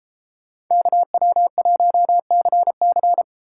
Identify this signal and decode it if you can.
Morse code, heard as a keyed tone, about 30 wpm, 700 Hz. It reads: KW1CC